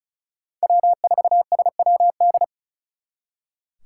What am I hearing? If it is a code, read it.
Morse code, heard as a keyed tone, about 45 wpm, 700 Hz. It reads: W4SWD